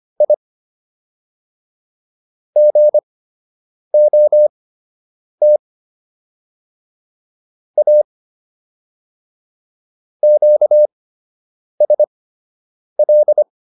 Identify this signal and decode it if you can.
Morse code, heard as a keyed tone, about 25 wpm, spaced as for 8 wpm, 600 Hz. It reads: I GOT A QSL